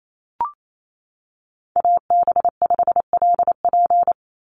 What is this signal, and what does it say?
Morse code, heard as a keyed tone, about 28 wpm, 700 Hz. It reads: AB5LP
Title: USA Call Signs 28wpm